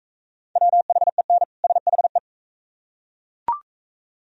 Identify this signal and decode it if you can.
Morse code, heard as a keyed tone, about 37 wpm, 700 Hz. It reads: WHEN SHE